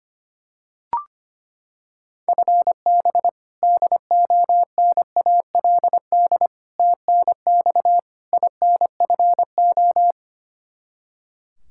Morse code, heard as a keyed tone, about 25 wpm, 700 Hz. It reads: FB DONALD TNX INFO
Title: Practice QSO - Exchange 3 - More Info - 25wpm